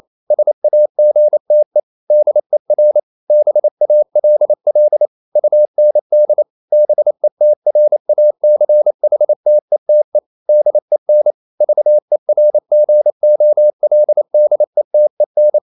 Morse code, heard as a keyed tone, about 28 wpm, 600 Hz. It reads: SAGTE DER BALL UND BETRACHTETE DEN VERGOLDETEN
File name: Liebespaar4 28wpm